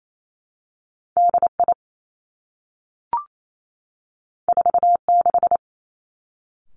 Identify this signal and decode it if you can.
Morse code, heard as a keyed tone, about 28 wpm, 700 Hz. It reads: D I 4 6
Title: Two Challenging Letter-Number Combinations 28wpm